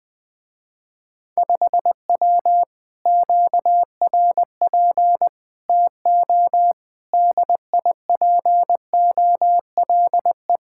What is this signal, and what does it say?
Morse code, heard as a keyed tone, about 20 wpm, 700 Hz. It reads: 5W QRP TO DIPOLE
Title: Speed-Racing Common QSO Phrases Encore - 20wpm